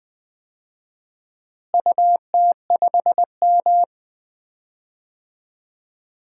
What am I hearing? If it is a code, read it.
Morse code, heard as a keyed tone, about 20 wpm, 700 Hz. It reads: UT5M
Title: DXCC-100 Call Signs - 4 Characters - Repeated 3x - 20wpm